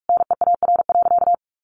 Morse code, heard as a keyed tone, about 44 wpm, 700 Hz. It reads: NEAR.
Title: Sentences from Top 300 Words 45wpm